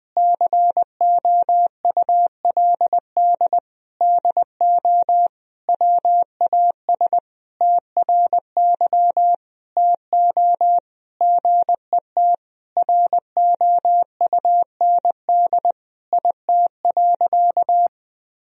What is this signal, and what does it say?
Morse code, heard as a keyed tone, about 20 wpm, 700 Hz. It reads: COULD DO WAS TRY TO GET ROUND IT.
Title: Sentences from Top 1000 Words - Review Part 1 - 20wpm